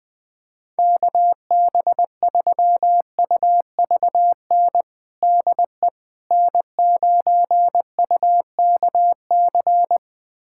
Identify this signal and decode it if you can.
Morse code, heard as a keyed tone, about 20 wpm, 700 Hz. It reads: KB3UVN DE N9UKC
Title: USA Callsign DE Callsign 20wpm